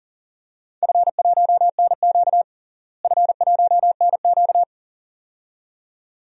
Morse code, heard as a keyed tone, about 40 wpm, 700 Hz. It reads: F1NQ F1NQ
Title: DXCC-100 Call Signs - 4 Characters - Repeated 2x - 40wpm